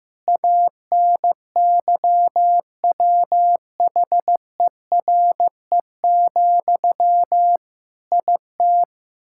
Morse code, heard as a keyed tone, about 15 wpm, 700 Hz. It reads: ANYWHERE, IT
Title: Sentences from Top 1600 Words 15wpm